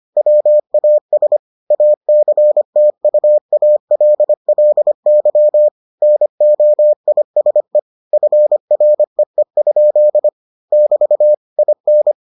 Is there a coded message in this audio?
Yes: Morse code from 0.2 to 12.1 s, heard as a keyed tone, about 25 wpm, 600 Hz. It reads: WAS ACTUALLY NOISE FREE? <BT> IN